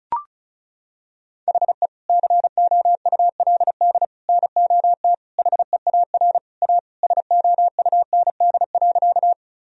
Morse code, heard as a keyed tone, about 34 wpm, 700 Hz. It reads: HE COULD NOT HEAR A SOUND.
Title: Sentences from Top 1000 Words - Review Part 1 - 35wpm